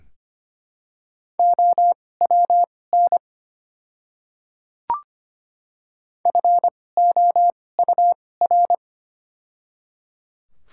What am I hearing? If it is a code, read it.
Morse code, heard as a keyed tone, about 25 wpm, spaced as for 18 wpm, 700 Hz. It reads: OWN FOUR
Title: 2x Character Spacing - Top 300 Words - 25wpm